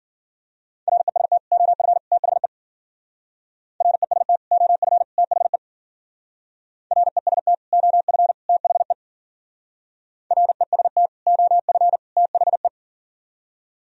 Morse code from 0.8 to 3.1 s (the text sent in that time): REST OF THE